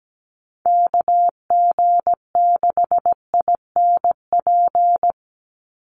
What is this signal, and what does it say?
Morse code, heard as a keyed tone, about 17 wpm, 700 Hz. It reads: KG6INP
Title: USA Call Signs 17wpm